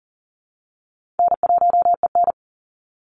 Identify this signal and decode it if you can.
Morse code, heard as a keyed tone, about 40 wpm, 700 Hz. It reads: N1EN
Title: Easy Call Signs Repeated 2x - 40wpm